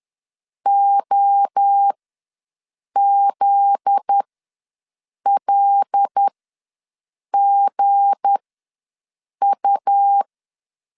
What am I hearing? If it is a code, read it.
Morse code, heard as a keyed tone, about 10 wpm, spaced as for 6 wpm, 800 Hz. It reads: OZLGU
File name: LEZIONI / 9 28 ws 10wpm 4124